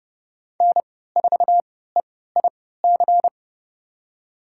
Morse code, heard as a keyed tone, about 30 wpm, spaced as for 17 wpm, 700 Hz. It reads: N4EIC
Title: USA Call Signs - 5 Characters - 3x Character Spacing - 30wpm